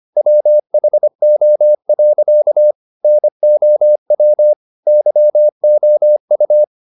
Morse code, heard as a keyed tone, about 25 wpm, 600 Hz. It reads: WHO. NOW YOU